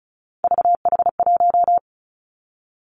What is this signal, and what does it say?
Morse code, heard as a keyed tone, about 36 wpm, 700 Hz. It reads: VH1
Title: Three Challenging Letter-Number Combinations 35wpm